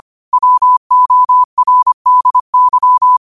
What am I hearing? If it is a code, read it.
Morse code, heard as a keyed tone, about 25 wpm, 1000 Hz. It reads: WORDY
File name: qotdfiles / 25wpm